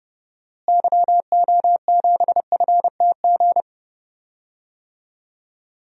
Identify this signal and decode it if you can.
Morse code, heard as a keyed tone, about 30 wpm, 700 Hz. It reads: YO7FTG
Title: DXCC-100 Call Signs - 6 Characters - Repeated 3x - 30wpm